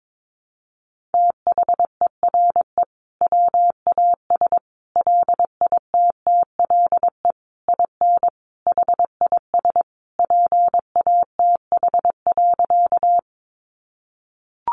Morse code, heard as a keyed tone, about 22 wpm, 700 Hz. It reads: THERE WAS LITTLE IN HIS PATH.
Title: Sentences from Top 1400 Words 22wpm